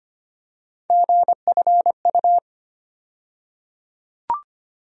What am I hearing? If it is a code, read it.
Morse code, heard as a keyed tone, about 25 wpm, 700 Hz. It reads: GFU
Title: USA Call Sign Postfixes - 3 Letters 25wpm